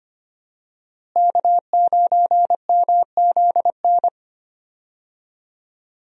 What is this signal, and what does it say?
Morse code, heard as a keyed tone, about 25 wpm, 700 Hz. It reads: K9MZN
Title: USA Call Signs - 5 Characters - Repeated 3x - 25wpm